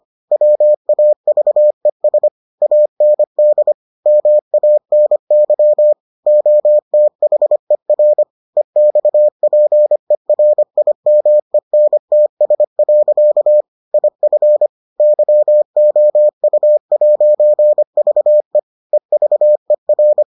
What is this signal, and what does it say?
Morse code, heard as a keyed tone, about 25 wpm, 600 Hz. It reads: WAVES AND MANY OTHER EXPERIMENTS. IF YOU'VE EVER